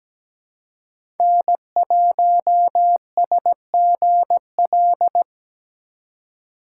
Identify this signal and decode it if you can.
Morse code, heard as a keyed tone, about 17 wpm, 700 Hz. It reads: N1SGL
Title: USA Call Signs - 5 Characters - 2x Character Spacing - 17wpm